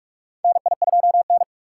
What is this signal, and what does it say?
Morse code, heard as a keyed tone, about 42 wpm, 700 Hz. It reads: NI2N
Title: USA Call Signs - 4 Characters - 6x Character Spacing - 45wpm